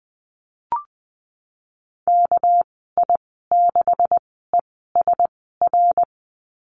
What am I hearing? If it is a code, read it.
Morse code, heard as a keyed tone, about 20 wpm, spaced as for 15 wpm, 700 Hz. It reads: KI6ESR
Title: 200 Lesson 32g - Letter k - Call Signs 20wpm